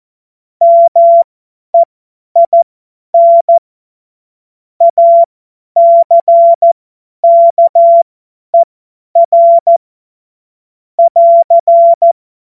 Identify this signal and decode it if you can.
Morse code, heard as a keyed tone, about 14 wpm, spaced as for 10 wpm, 680 Hz. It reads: MEIN ACKER <AR>
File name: Witz-09-September-2025-50-F70